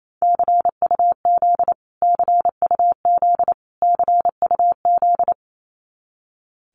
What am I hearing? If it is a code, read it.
Morse code, heard as a keyed tone, about 28 wpm, 700 Hz. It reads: CUZ CUZ CUZ
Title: Abbreviations Repeated 3x - 28wpm